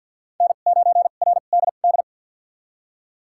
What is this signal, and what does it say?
Morse code, heard as a keyed tone, about 52 wpm, spaced as for 36 wpm, 700 Hz. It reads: N9RDD
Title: USA Call Signs - 5 Characters - 2x Character Spacing - 50wpm